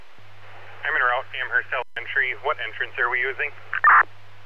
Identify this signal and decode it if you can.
Emergency Alert System (SAME) tones, heard as harsh buzzing bursts, with speech. It is heard from 0.7 to 3.5 s.